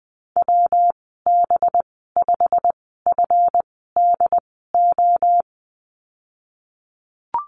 Morse code, heard as a keyed tone, about 19 wpm, spaced as for 14 wpm, 700 Hz. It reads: WB5FDO
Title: 238 Lesson 38g - Letter q - Call Signs 20wpm